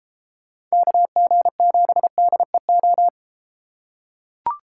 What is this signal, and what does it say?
Morse code, heard as a keyed tone, about 31 wpm, 700 Hz. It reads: KG7DEO